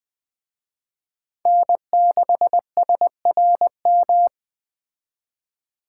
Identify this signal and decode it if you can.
Morse code, heard as a keyed tone, about 21 wpm, 700 Hz. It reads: N6SRM
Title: Speed Racing - USA 1x3 Call Signs 17wpm